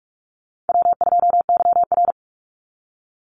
Morse code, heard as a keyed tone, about 44 wpm, 700 Hz. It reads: W2YR
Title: Call Signs - 45wpm